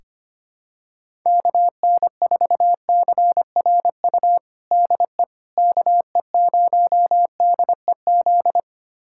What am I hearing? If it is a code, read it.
Morse code, heard as a keyed tone, about 25 wpm, 700 Hz. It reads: KN4CRU DE KE0DEZ
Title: USA Callsign DE Callsign - 6 Characters - 25wpm